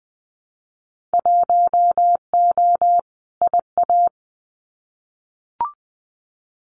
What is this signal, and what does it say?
Morse code, heard as a keyed tone, about 20 wpm, 700 Hz. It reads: 1O IA
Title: Winter Field Day Exchanges 20wpm